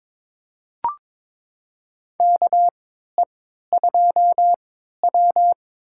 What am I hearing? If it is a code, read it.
Morse code, heard as a keyed tone, about 22 wpm, spaced as for 12 wpm, 700 Hz. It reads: KE2W
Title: USA Call Signs - 3x Character Spacing - 22wpm